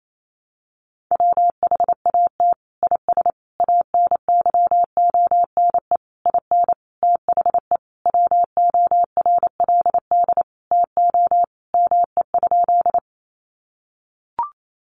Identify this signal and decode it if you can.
Morse code, heard as a keyed tone, about 28 wpm, 700 Hz. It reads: WHAT IS ANYONE IN THE WORLD TO ME?
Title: Sentences from Top 900 Words 28wpm